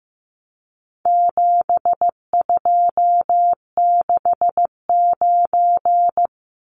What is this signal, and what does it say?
Morse code, heard as a keyed tone, about 15 wpm, 700 Hz. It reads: 7269
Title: Numbers - 4-Digit 15wpm